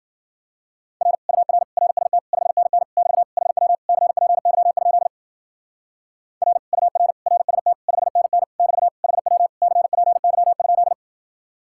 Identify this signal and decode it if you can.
Morse code, heard as a keyed tone, about 50 wpm, 700 Hz. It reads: R UR RST 5NN <BT> HW CPY? R UR RST 5NN <BT> HW CPY?